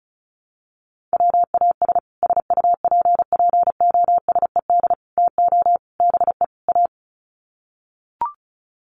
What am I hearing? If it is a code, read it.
Morse code, heard as a keyed tone, about 34 wpm, 700 Hz. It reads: WAS SUPPOSED TO BE A
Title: Sets of 5 Words Encore 35wpm